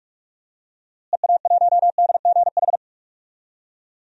Morse code, heard as a keyed tone, about 46 wpm, 700 Hz. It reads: EA1DGH
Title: DXCC-100 Call Signs - 6 Characters - Repeated 3x - 45wpm